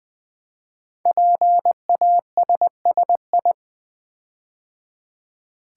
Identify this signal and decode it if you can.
Morse code, heard as a keyed tone, about 20 wpm, 700 Hz. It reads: PASSI
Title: Build A Word - 5 Letters At A Time - 10-Letter Words - 20wpm